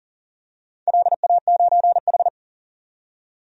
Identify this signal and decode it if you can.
Morse code, heard as a keyed tone, about 40 wpm, 700 Hz. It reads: LA9H
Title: DXCC-100 Call Signs - 4 Characters - 40wpm